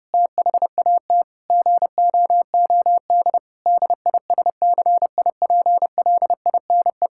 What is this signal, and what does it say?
Morse code, heard as a keyed tone, about 30 wpm, 700 Hz. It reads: THAT GOOD DISCIPLINE
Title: Sentences from Top 1000 Words - Review Part 6 - 30wpm